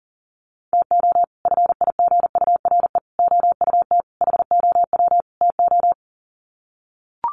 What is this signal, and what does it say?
Morse code, heard as a keyed tone, about 40 wpm, 700 Hz. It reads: TO FIGURE OUT HOW TO
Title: Sets of 5 Words 40wpm